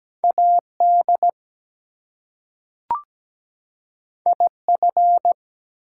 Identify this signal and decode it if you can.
Morse code, heard as a keyed tone, about 17 wpm, 700 Hz. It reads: AD IF
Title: Two Letter Words - 17wpm